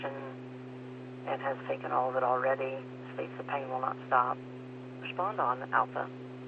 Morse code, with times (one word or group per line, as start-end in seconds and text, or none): none